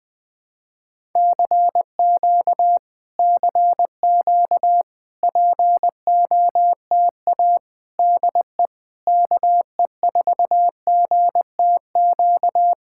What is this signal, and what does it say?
Morse code, heard as a keyed tone, about 20 wpm, 700 Hz. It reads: CQ CQ POTA DE KE4GTQ
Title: POTA QSO Practice - 20wpm